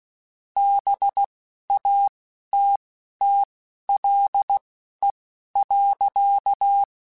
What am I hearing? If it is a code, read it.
Morse code, heard as a keyed tone, about 16 wpm, spaced as for 11 wpm, 800 Hz. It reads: BATTLE.
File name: Quotes Spaced / Quotes Spaced-mp3s / Pessimism sp @15